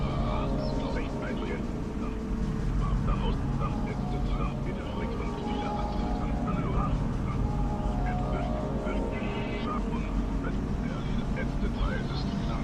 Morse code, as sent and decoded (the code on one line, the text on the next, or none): none